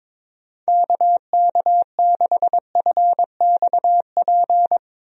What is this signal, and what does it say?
Morse code, heard as a keyed tone, about 22 wpm, 700 Hz. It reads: KK6FXP
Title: USA Call Signs - 6 Characters - Repeated 2x - 22wpm